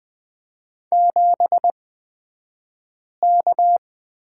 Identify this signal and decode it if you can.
Morse code, heard as a keyed tone, about 20 wpm, spaced as for 5 wpm, 700 Hz. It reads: 7K
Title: 254 Lesson 41d - Period - Two Characters 20wpm